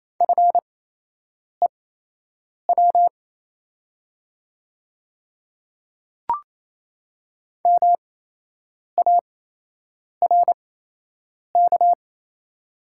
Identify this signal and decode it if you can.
Morse code, heard as a keyed tone, about 28 wpm, spaced as for 8 wpm, 700 Hz. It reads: FEW MARK
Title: 8x Character Spacing - Top 500 Words - 28wpm